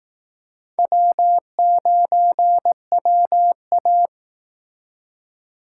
Morse code, heard as a keyed tone, about 18 wpm, 700 Hz. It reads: W9WA
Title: Speed Racing - USA Call Signs 15wpm